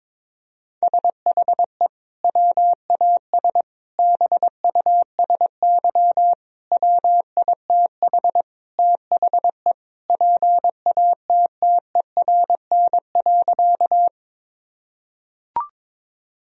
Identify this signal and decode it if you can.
Morse code, heard as a keyed tone, about 22 wpm, 700 Hz. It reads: SHE WAS BUSY WITH THE PATTERN.
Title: Sentences from Top 1000 Words - Review Part 2 - 22wpm